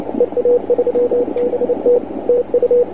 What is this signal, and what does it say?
Morse code, heard as a keyed tone, about 29 wpm, 490 Hz, with radio noise. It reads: EA3BT TU
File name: dxrecords / vp8thu 28cw